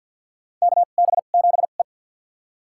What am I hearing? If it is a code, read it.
Morse code, heard as a keyed tone, about 53 wpm, spaced as for 37 wpm, 700 Hz. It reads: KB7E
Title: USA Call Signs - 2x Character Spacing - 50wpm